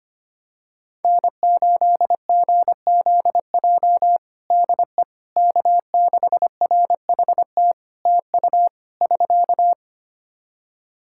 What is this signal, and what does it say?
Morse code, heard as a keyed tone, about 25 wpm, 700 Hz. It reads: N8GZJ DE K6RHT TU <SK>
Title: Practice QSO - Exchange 4 - Ending - 25wpm